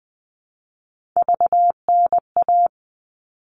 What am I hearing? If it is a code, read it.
Morse code, heard as a keyed tone, about 20 wpm, 700 Hz. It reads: VNA
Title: All Cut Numbers - 3-Digit 20wpm